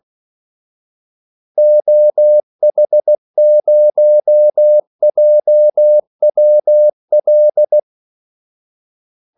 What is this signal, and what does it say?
Morse code, heard as a keyed tone, about 16 wpm, 600 Hz. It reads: OH0JWL
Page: Kaldesignaler 16 wpm